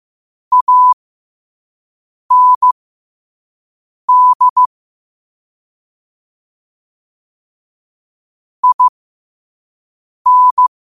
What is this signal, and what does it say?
Morse code, heard as a keyed tone, about 15 wpm, spaced as for 5 wpm, 1000 Hz. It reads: AND IN
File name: qotdfiles / 5wpm